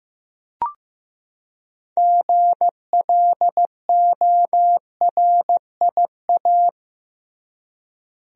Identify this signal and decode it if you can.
Morse code, heard as a keyed tone, about 15 wpm, 700 Hz. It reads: GLORIA
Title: USA Common Names - 15wpm